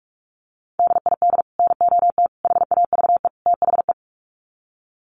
Morse code, heard as a keyed tone, about 44 wpm, 700 Hz. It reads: DID NOT HAVE THE
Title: Sets of 4 Words 45wpm